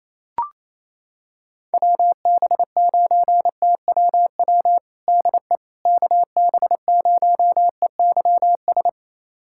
Morse code, heard as a keyed tone, about 28 wpm, 700 Hz. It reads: WB9TWW DE KB0EYS
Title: USA Callsign DE Callsign - 6 Characters - 28wpm